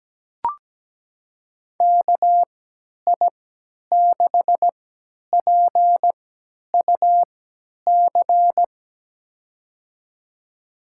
Morse code, heard as a keyed tone, about 17 wpm, spaced as for 10 wpm, 700 Hz. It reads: KI6PUC
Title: USA Call Signs - 6 Characters - 3x Character Spacing - 17wpm